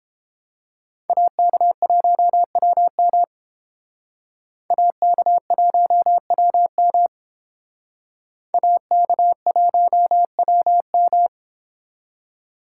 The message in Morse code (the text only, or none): AK1WM AK1WM AK1WM